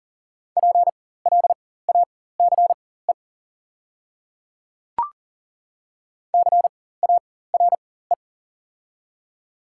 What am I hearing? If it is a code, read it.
Morse code, heard as a keyed tone, about 40 wpm, spaced as for 19 wpm, 690 Hz. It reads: PLACE CARE
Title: 4x Character Spacing - Top 500 Words - 40wpm